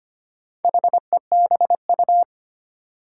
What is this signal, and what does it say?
Morse code, heard as a keyed tone, about 25 wpm, 700 Hz. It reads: HEBU
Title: Four Challenging Letter-Number Combinations 25wpm